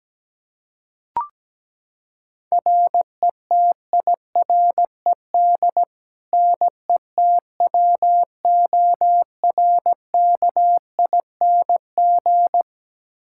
Morse code, heard as a keyed tone, about 17 wpm, 700 Hz. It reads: RETIRED NETWORKING
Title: K1USN SST QSO Practice 17wpm